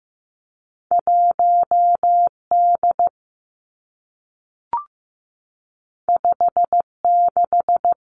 Morse code, heard as a keyed tone, about 15 wpm, 700 Hz. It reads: 1 D 5 6
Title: Two Challenging Letter-Number Combinations 15wpm